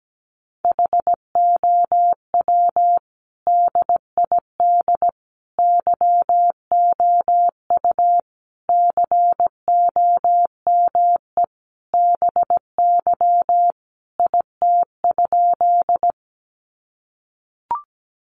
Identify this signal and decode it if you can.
Morse code, heard as a keyed tone, about 17 wpm, 700 Hz. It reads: HOW DID YOU COME BY IT?
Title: Sentences from Top 100 Words 17wpm